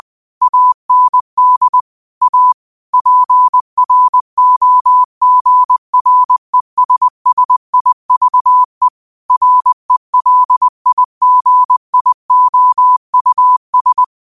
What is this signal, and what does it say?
Morse code, heard as a keyed tone, about 20 wpm, 1000 Hz. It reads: AND A PROGRESSIVE RELIGIOUS